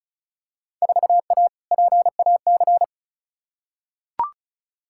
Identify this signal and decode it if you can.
Morse code, heard as a keyed tone, about 36 wpm, 690 Hz. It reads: 4A PAC
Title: Field Day Exchanges Encore - 35wpm